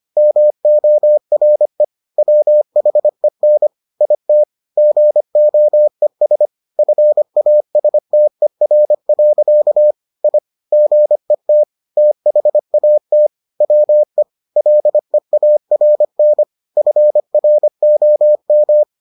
Morse code, heard as a keyed tone, about 25 wpm, 600 Hz. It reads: MORE WHEN IT GOES FASTER. I GET THAT WE LEARN FROM